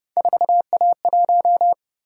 Morse code, heard as a keyed tone, about 29 wpm, 700 Hz. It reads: 4A1